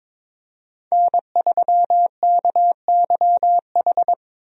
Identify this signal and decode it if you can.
Morse code, heard as a keyed tone, about 22 wpm, 700 Hz. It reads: N3KYH
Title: USA Call Signs - 5 Characters - Repeated 2x - 22wpm